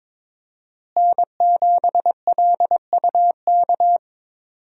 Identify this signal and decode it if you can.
Morse code, heard as a keyed tone, about 22 wpm, 700 Hz. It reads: N7LUK
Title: USA Call Signs - 5 Characters - Repeated 3x - 22wpm